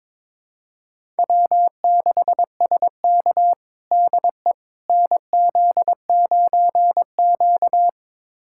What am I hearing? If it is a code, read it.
Morse code, heard as a keyed tone, about 21 wpm, 700 Hz. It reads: W6SK DE NZ9Q
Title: USA Callsign DE Callsign - 4 Characters - 22wpm